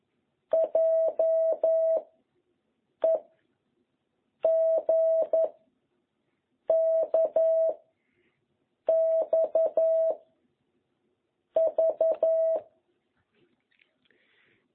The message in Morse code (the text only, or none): J E G K X V